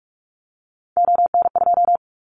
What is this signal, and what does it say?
Morse code, heard as a keyed tone, about 44 wpm, 700 Hz. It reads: ON2
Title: Callsign Prefixes for Top 100 DXCC - 45wpm